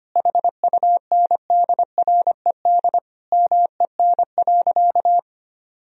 Morse code, heard as a keyed tone, about 25 wpm, 700 Hz. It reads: HUNDRED MEN.
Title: Sentences from Top 1000 Words - Review Part 2 - 25wpm